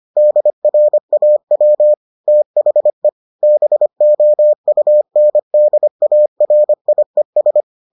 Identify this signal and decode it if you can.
Morse code, heard as a keyed tone, about 25 wpm, 600 Hz. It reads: DRAW THE BOUNDARIES